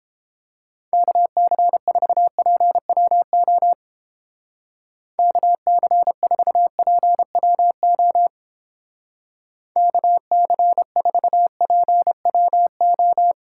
Morse code, heard as a keyed tone, about 30 wpm, 700 Hz. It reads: KC4PWO KC4PWO KC4PWO